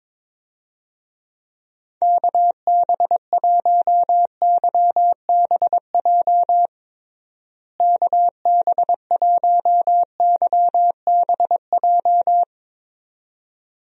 Morse code, heard as a keyed tone, about 22 wpm, 700 Hz. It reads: KB1YBJ KB1YBJ
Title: USA Call Signs - 6 Characters - Repeated 2x - 22wpm